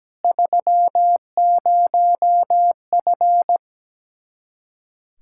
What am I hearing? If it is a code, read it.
Morse code, heard as a keyed tone, about 17 wpm, 700 Hz. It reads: 30F
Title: K1USN SST QSO Practice 17wpm